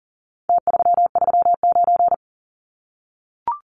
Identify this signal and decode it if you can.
Morse code, heard as a keyed tone, about 40 wpm, 700 Hz. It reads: T339
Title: Numbers - Contest Style - 4-Digit 40wpm